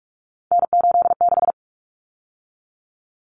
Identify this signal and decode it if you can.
Morse code, heard as a keyed tone, about 44 wpm, 700 Hz. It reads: N86
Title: Common Cut Numbers - 3-Digit 45wpm